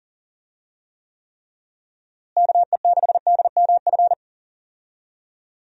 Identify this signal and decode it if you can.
Morse code, heard as a keyed tone, about 40 wpm, 700 Hz. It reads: KE6DMF
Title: USA Call Signs - 6 Characters - Repeated 3x - 40wpm